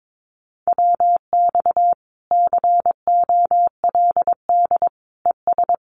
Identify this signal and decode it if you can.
Morse code, heard as a keyed tone, about 22 wpm, 700 Hz. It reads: WX COLD ES
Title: Common QSO Phrases Encore 22wpm